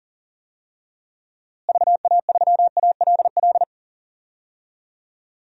Morse code, heard as a keyed tone, about 40 wpm, 700 Hz. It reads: VA3ALL
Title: DXCC-100 Call Signs - 6 Characters - Repeated 2x - 40wpm